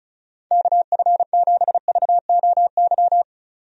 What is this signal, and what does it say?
Morse code, heard as a keyed tone, about 34 wpm, 700 Hz. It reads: KF7VOY
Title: USA Call Signs - 6 Characters - 35wpm